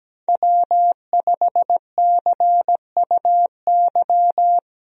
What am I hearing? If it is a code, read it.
Morse code, heard as a keyed tone, about 17 wpm, 700 Hz. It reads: W5CUY
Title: USA Call Signs - 5 Characters - Repeated 2x - 17wpm